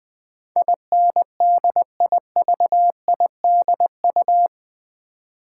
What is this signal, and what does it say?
Morse code, heard as a keyed tone, about 20 wpm, 700 Hz. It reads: INDIVIDU
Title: Build A Word - 4 Letters At A Time - 10-Letter Words - 20wpm